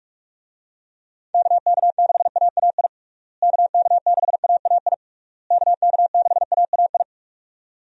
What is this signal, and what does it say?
Morse code, heard as a keyed tone, about 45 wpm, 700 Hz. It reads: KK6AAI KK6AAI KK6AAI
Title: USA Call Signs - 6 Characters - Repeated 3x - 45wpm